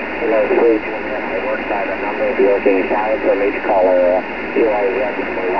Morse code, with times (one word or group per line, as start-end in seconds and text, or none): none